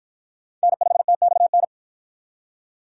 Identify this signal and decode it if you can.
Morse code, heard as a keyed tone, about 53 wpm, 700 Hz. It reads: N5TXN